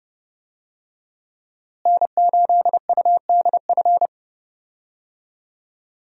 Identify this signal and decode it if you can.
Morse code, heard as a keyed tone, about 30 wpm, 700 Hz. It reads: N8UDF
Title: USA Call Signs - 5 Characters - Repeated 2x - 30wpm